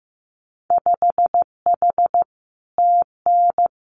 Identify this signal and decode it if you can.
Morse code, heard as a keyed tone, about 15 wpm, 700 Hz. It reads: 5H TN
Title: Winter Field Day Exchanges Encore 15wpm